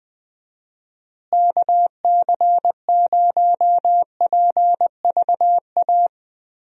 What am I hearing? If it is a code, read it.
Morse code, heard as a keyed tone, about 20 wpm, 700 Hz. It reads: KC0PVA
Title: USA Call Signs - 6 Characters - Repeated 2x - 20wpm